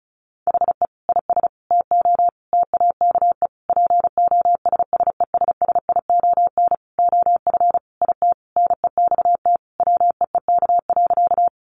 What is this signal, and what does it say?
Morse code, heard as a keyed tone, about 34 wpm, 700 Hz. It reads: HE IS TO TAKE POSSESSION OF IT NEXT WEEK.
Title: Sentences from Top 1400 Words 35wpm